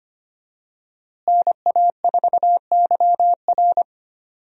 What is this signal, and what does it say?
Morse code, heard as a keyed tone, about 25 wpm, 700 Hz. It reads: NA4YR
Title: USA Call Signs - 5 Characters - Repeated 3x - 25wpm